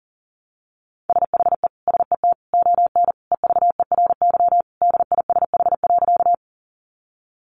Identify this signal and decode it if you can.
Morse code, heard as a keyed tone, about 40 wpm, 700 Hz. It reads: SHE SET ON EVERY DISH.
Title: Sentences from Top 1300 Words 40wpm